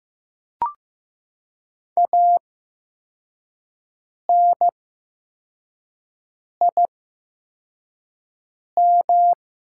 Morse code, heard as a keyed tone, about 15 wpm, spaced as for 4 wpm, 700 Hz. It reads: ANIM
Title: 8x Character Spacing - Top 200 Words - 15wpm